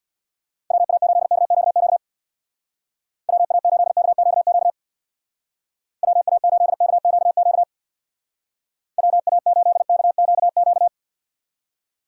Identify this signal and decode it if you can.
Morse code, heard as a keyed tone, about 58 wpm, 700 Hz. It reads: WA8KQQ WA8KQQ WA8KQQ WA8KQQ
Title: Speed Racing - USA Call Signs 50wpm